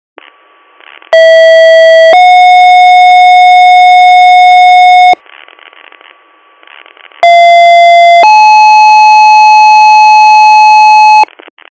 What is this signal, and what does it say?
Two-tone page, heard as a one-second tone, then a three-second tone, about 650 then 730 Hz, about 690 then 860 Hz. It is heard from 1.1 to 11.2 s.